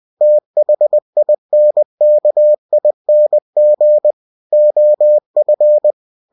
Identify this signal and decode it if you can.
Morse code, heard as a keyed tone, about 20 wpm, 600 Hz. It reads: THINKING OF
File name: cw-books / winnie-the-pooh / winnie-20wpm-page-0021